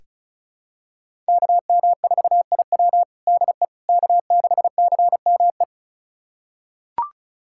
Morse code, heard as a keyed tone, about 34 wpm, 690 Hz. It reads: KM4IW DE K6CME
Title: USA Callsign DE Callsign - 5 Characters - 35wpm